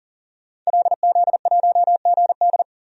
Morse code, heard as a keyed tone, about 40 wpm, 700 Hz. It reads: LZ1GD
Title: DXCC-100 Call Signs - 5 Characters - 40wpm